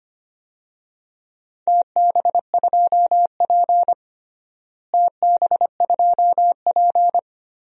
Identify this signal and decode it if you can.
Morse code, heard as a keyed tone, about 25 wpm, 700 Hz. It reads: TB2P TB2P
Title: DXCC-100 Call Signs - 4 Characters - Repeated 2x - 25wpm